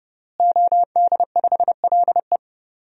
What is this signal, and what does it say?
Morse code, heard as a keyed tone, about 30 wpm, 700 Hz. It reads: OD5LE
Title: DXCC-100 Call Signs - 5 Characters - Repeated 2x - 30wpm